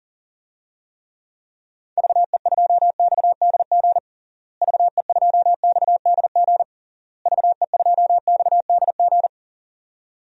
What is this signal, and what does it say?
Morse code, heard as a keyed tone, about 37 wpm, 700 Hz. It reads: VE2XDG VE2XDG VE2XDG
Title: DXCC-100 Call Signs - 6 Characters - Repeated 3x - 40wpm